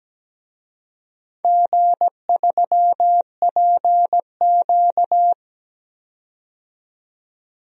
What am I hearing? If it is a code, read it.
Morse code, heard as a keyed tone, about 17 wpm, 700 Hz. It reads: G3PQ
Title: DXCC-100 Call Signs - 4 Characters - 17wpm